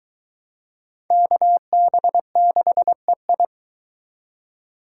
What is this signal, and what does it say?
Morse code, heard as a keyed tone, about 20 wpm, 700 Hz. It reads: KB6EI